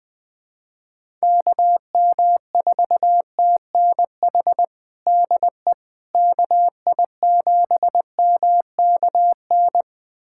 Morse code, heard as a keyed tone, about 20 wpm, 700 Hz. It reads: KM4TNH DE KI7MKN
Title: USA Callsign DE Callsign - 6 Characters - 20wpm